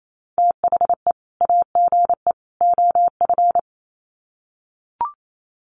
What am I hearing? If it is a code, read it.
Morse code, heard as a keyed tone, about 28 wpm, 700 Hz. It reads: THE AGE OF
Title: Sets of 3 Words 28wpm